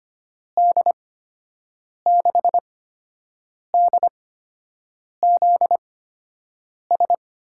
Morse code, heard as a keyed tone, about 25 wpm, spaced as for 7 wpm, 700 Hz. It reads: D6DZS
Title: USA Call Signs - 6 Characters - 8x Character Spacing - 25wpm